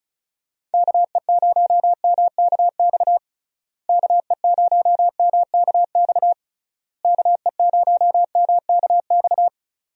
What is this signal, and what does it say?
Morse code, heard as a keyed tone, about 36 wpm, 700 Hz. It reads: KE0MKX KE0MKX KE0MKX
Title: USA Call Signs - 6 Characters - Repeated 3x - 35wpm